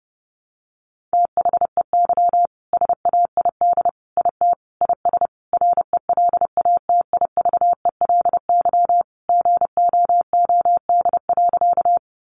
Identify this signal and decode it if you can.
Morse code, heard as a keyed tone, about 30 wpm, 700 Hz. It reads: THEY SAID IT IS RELATIVELY GOOD.
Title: Sentences from Top 1100 Words 30wpm